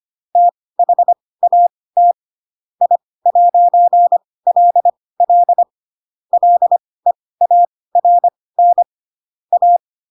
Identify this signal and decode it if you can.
Morse code, heard as a keyed tone, about 25 wpm, spaced as for 18 wpm, 700 Hz. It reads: THAT I'LL LEARN A